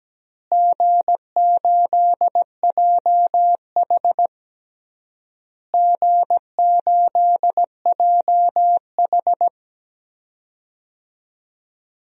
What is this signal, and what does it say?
Morse code, heard as a keyed tone, about 17 wpm, 700 Hz. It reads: G8JH G8JH
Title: DXCC-100 Call Signs - 4 Characters - Repeated 3x - 17wpm